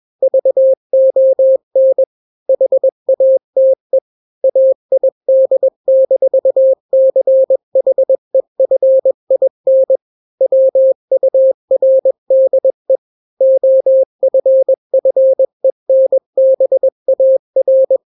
Morse code, heard as a keyed tone, about 21 wpm, 530 Hz. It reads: VON HATE AID-CHEFIN WURDE OFFENBAR
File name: daily / 20wpm